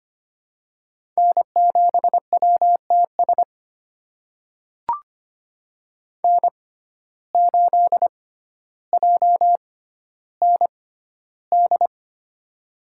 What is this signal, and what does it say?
Morse code, heard as a keyed tone, about 25 wpm, spaced as for 13 wpm, 700 Hz. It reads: N7WTS N8JND
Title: USA Call Signs - 5 Characters - 6x Character Spacing - 25wpm